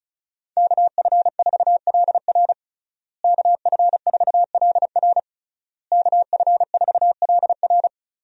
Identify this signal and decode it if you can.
Morse code, heard as a keyed tone, about 36 wpm, 700 Hz. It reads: KF4LR KF4LR KF4LR
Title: USA Call Signs - 5 Characters - Repeated 3x - 35wpm